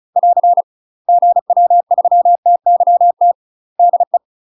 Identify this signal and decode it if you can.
Morse code, heard as a keyed tone, about 36 wpm, 700 Hz. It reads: <AR> GW3TYT DE
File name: UK MASTER QSOs TRAINING / 35WPM / UK MASTER QSOs 0014